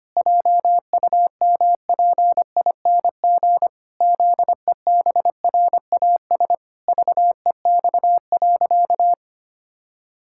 Morse code, heard as a keyed tone, about 25 wpm, 700 Hz. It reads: JUMPING ZEBRAS VEX.
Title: Pangram Challenge v2 - 25wpm